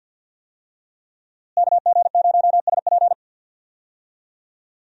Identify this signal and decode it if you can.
Morse code, heard as a keyed tone, about 51 wpm, 700 Hz. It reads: KG0SP
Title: USA Call Signs - 5 Characters - Repeated 3x - 50wpm